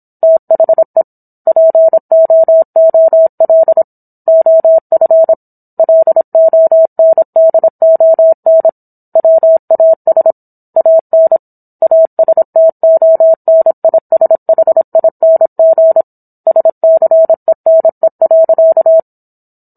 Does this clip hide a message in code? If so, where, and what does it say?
Yes: Morse code from 0.2 to 19.0 s, heard as a keyed tone, about 26 wpm, 650 Hz. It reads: THE POOL OF LONDON WAS AN ASTONISHING SCENE.